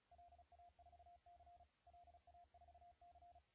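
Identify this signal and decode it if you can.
Morse code, heard as a keyed tone, about 36 wpm, 700 Hz, with radio noise. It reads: LA4C LA4C
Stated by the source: CW contest audio recorded at LA4C during the SAC CW contest.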